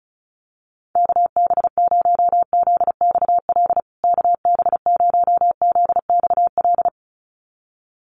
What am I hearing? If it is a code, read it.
Morse code, heard as a keyed tone, about 34 wpm, 700 Hz. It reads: KB0ZXL KB0ZXL
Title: USA Call Signs Repeated 2x 35wpm